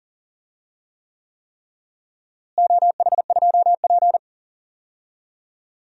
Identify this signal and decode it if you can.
Morse code, heard as a keyed tone, about 37 wpm, 700 Hz. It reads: OH2P
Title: DXCC-100 Call Signs - 4 Characters - Repeated 3x - 40wpm